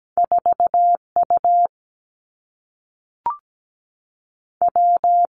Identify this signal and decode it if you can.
Morse code, heard as a keyed tone, about 17 wpm, 700 Hz. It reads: 4 U W
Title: Two Challenging Letter-Number Combinations 17wpm